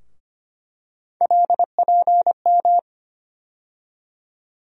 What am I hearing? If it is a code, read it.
Morse code, heard as a keyed tone, about 25 wpm, 700 Hz. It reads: LPM
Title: USA Call Sign Postfixes - 3 Letters 25wpm